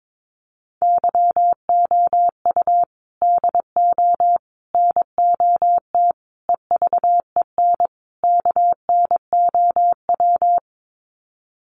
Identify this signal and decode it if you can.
Morse code, heard as a keyed tone, about 22 wpm, 700 Hz. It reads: YOU DO NOT EVEN KNOW
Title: Sets of 5 Words Encore2 22wpm